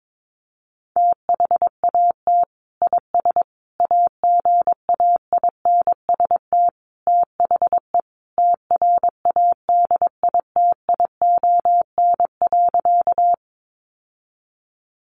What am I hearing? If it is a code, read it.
Morse code, heard as a keyed tone, about 22 wpm, 700 Hz. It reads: THAT IS AGAINST THE TRADITION.
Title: Sentences from Top 1500 Words 22wpm